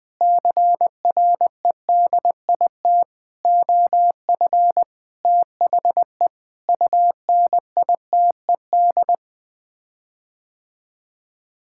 Morse code, heard as a keyed tone, about 21 wpm, 700 Hz. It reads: CREDIT OF THE UNITED
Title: Sets of 8 Words - 20wpm